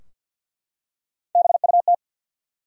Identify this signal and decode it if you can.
Morse code, heard as a keyed tone, about 53 wpm, 700 Hz. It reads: BUT
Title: Top 100 Words Repeated 50wpm